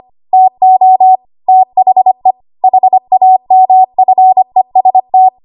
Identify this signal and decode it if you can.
Morse code, heard as a keyed tone, about 25 wpm, 750 Hz. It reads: TO THE HAMFEST
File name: Morse / 25WPM